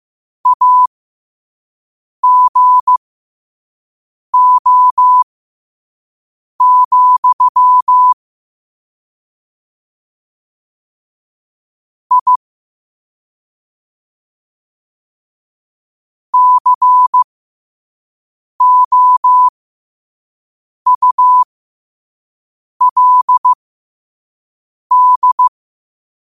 Morse code, heard as a keyed tone, about 15 wpm, spaced as for 5 wpm, 1000 Hz. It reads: AGO, I COULD
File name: qotdfiles / 5wpm